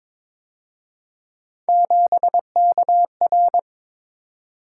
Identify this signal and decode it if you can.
Morse code, heard as a keyed tone, about 27 wpm, 700 Hz. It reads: 7KR
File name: Speed-Racing - Three Letter-Number Combinations - 22wpm